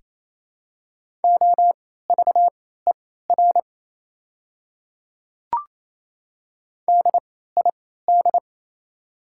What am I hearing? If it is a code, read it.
Morse code, heard as a keyed tone, about 28 wpm, spaced as for 16 wpm, 700 Hz. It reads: OVER DID
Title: 3x Character Spacing - Top 200 Words - 28wpm